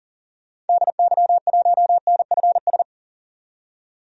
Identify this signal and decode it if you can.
Morse code, heard as a keyed tone, about 40 wpm, 700 Hz. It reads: DY1NFS
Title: DXCC-100 Call Signs - 6 Characters - 40wpm